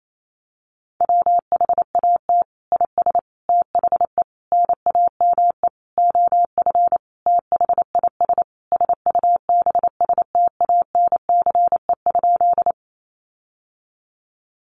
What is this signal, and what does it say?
Morse code, heard as a keyed tone, about 28 wpm, 700 Hz. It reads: WHAT IS THE NAME OF THIS SUBSTANCE?
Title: Sentences from Top 1500 Words 28wpm